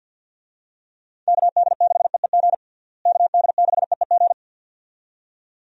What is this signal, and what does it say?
Morse code, heard as a keyed tone, about 48 wpm, 700 Hz. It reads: KD6EEG KD6EEG
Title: USA Call Signs - 6 Characters - Repeated 2x - 50wpm